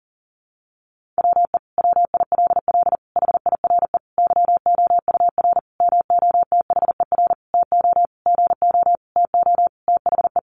Morse code, heard as a keyed tone, about 40 wpm, 700 Hz. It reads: WE WILL HIRE YOUR MOTHER TO GO TO THE
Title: Sentences from Top 1400 Words 40wpm